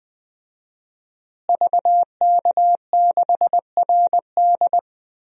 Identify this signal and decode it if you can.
Morse code, heard as a keyed tone, about 20 wpm, 700 Hz. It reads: VK6RD
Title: DXCC-100 Call Signs - 5 Characters - Repeated 3x - 20wpm